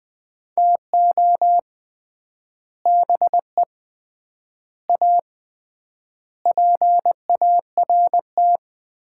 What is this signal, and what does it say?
Morse code, heard as a keyed tone, about 20 wpm, 700 Hz. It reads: TO BE A PART
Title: 3x Word-Spacing - Sets of 5 Words - 20wpm